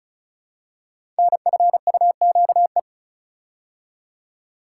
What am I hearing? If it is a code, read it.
Morse code, heard as a keyed tone, about 35 wpm, 700 Hz. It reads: NFUQE
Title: Code Groups - 5 Letter:Numbers - 35wpm